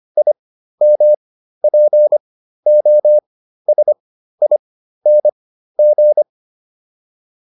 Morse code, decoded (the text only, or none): IMPOSING